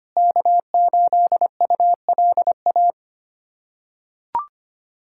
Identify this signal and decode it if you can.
Morse code, heard as a keyed tone, about 25 wpm, 700 Hz. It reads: K8ULA
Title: USA Callsign DE Callsign 25wpm